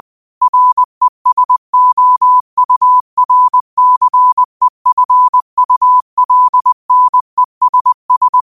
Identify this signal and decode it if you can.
Morse code, heard as a keyed tone, about 20 wpm, 1000 Hz. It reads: RESOURCEFULNESS
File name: qotdfiles / 20wpm